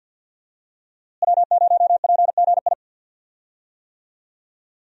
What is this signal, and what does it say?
Morse code, heard as a keyed tone, about 51 wpm, 700 Hz. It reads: W0PGI
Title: USA Call Signs - 5 Characters - Repeated 3x - 50wpm